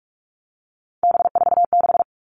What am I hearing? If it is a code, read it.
Morse code, heard as a keyed tone, about 44 wpm, 700 Hz. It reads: B46
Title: All Cut Numbers - 3-Digit 45wpm